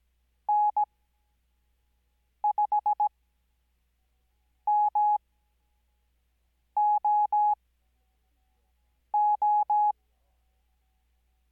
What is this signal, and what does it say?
Morse code, heard as a keyed tone, about 17 wpm, spaced as for 5 wpm, 840 Hz. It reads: N5MOO
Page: Farnsworth Spacing: 15WPM Character speed Spaced at 5WPM.